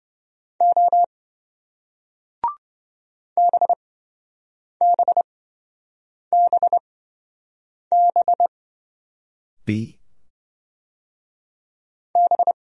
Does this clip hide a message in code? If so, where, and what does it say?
Yes: Morse code from 0.6 to 12.5 s, heard as a keyed tone, about 28 wpm, spaced as for 7 wpm, 700 Hz. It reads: O BBBB B